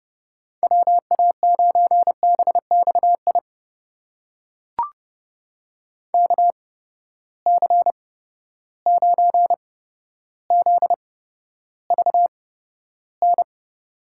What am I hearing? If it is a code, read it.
Morse code, heard as a keyed tone, about 30 wpm, spaced as for 13 wpm, 700 Hz. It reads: WA9BXI KC9ZVN
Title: USA Call Signs - 6 Characters - 8x Character Spacing - 30wpm